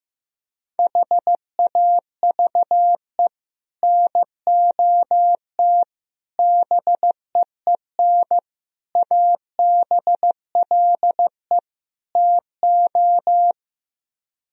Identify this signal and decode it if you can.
Morse code, heard as a keyed tone, about 15 wpm, 700 Hz. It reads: HAVE NOT BEEN ABLE TO
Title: Speed-Racing - Sets of 5 Words 15wpm